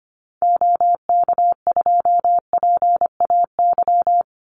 Morse code, heard as a keyed tone, about 25 wpm, 700 Hz. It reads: OK2PAY
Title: Call Signs Repeated 3x - 25wpm